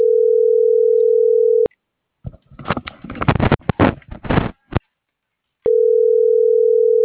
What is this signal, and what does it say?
Ringback tone (North America), 440 + 480 Hz.